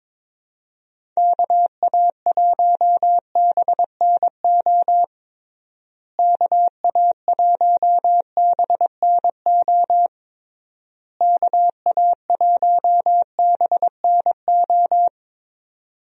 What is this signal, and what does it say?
Morse code, heard as a keyed tone, about 22 wpm, 700 Hz. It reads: KA1BNO KA1BNO KA1BNO
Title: USA Call Signs - 6 Characters - Repeated 3x - 22wpm